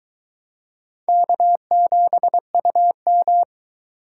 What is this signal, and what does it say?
Morse code, heard as a keyed tone, about 22 wpm, 700 Hz. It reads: K7UM